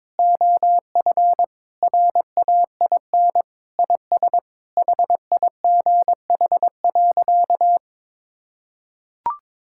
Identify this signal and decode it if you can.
Morse code, heard as a keyed tone, about 22 wpm, 700 Hz. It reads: OF RAIN IS HIGH.
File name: practice / 22wpm / Sentences with 3-Word Phrases - 22wpm